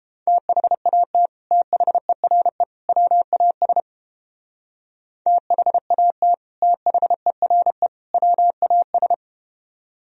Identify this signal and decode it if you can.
Morse code, heard as a keyed tone, about 32 wpm, 700 Hz. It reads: THAT THERE WAS THAT THERE WAS